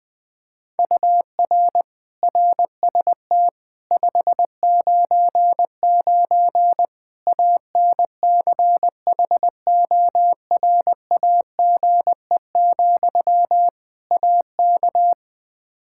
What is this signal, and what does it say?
Morse code, heard as a keyed tone, about 20 wpm, 700 Hz. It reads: UR RST 599 ANCHORAGE, AK
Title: K1USN SST QSO Practice 20wpm